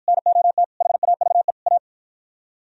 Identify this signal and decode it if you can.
Morse code, heard as a keyed tone, about 51 wpm, 700 Hz. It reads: NOT HAVE A